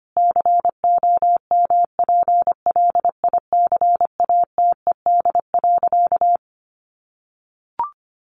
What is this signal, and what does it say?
Morse code, heard as a keyed tone, about 25 wpm, 700 Hz. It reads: COMPLICATED.
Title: Sentences from Top 1200 Words 25wpm